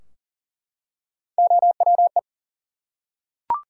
Morse code, heard as a keyed tone, about 40 wpm, 700 Hz. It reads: OWE
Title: Three Letter Words - 40wpm